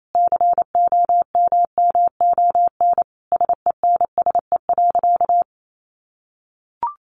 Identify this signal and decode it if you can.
Morse code, heard as a keyed tone, about 28 wpm, 700 Hz. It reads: COMMON SENSE.
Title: Sentences from Top 1300 Words 28wpm